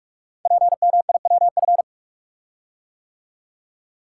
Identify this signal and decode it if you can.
Morse code, heard as a keyed tone, about 44 wpm, 700 Hz. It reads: PMIWF
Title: Code Groups - 5 Letter:Numbers - 45wpm